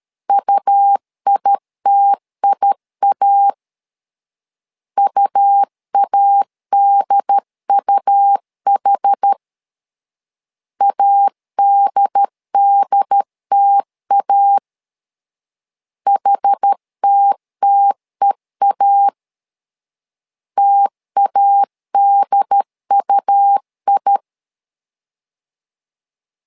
Morse code, heard as a keyed tone, about 13 wpm, 800 Hz. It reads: UITIA UADUH ADDTA HTTEA TADUI
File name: LEZIONI / 2 UHDN / 2 UHD 12wpm 1540